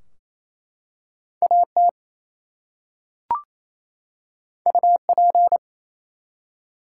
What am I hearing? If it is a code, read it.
Morse code, heard as a keyed tone, about 27 wpm, 700 Hz. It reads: AT UP
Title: Two Letter Words - 28wpm